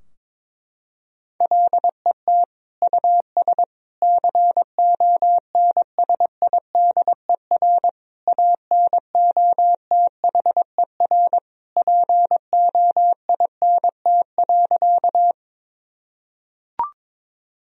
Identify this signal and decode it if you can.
Morse code, heard as a keyed tone, about 22 wpm, 700 Hz. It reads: LET US CONSIDER ANOTHER POINT.
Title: Sentences from Top 1000 Words - Review Part 4 - 22wpm